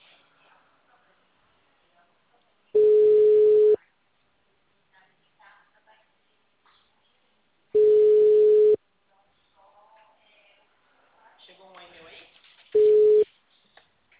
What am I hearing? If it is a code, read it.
Ringback tone (Europe, 425 Hz), with speech.
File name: out- -2007- - - .38197